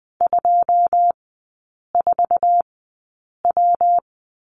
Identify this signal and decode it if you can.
Morse code, heard as a keyed tone, about 20 wpm, spaced as for 8 wpm, 690 Hz. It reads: 24W
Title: 198 Lesson 32e - Letter k - Three Characters 20wpm